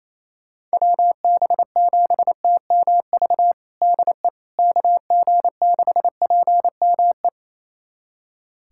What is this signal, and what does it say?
Morse code, heard as a keyed tone, about 28 wpm, 700 Hz. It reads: WB7TMV DE KG6PME
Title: USA Callsign DE Callsign 28wpm